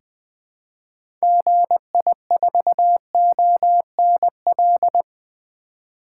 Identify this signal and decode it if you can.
Morse code, heard as a keyed tone, about 20 wpm, 700 Hz. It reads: GI4ONL
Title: 2023 SOTA Activator Call Signs - 20wpm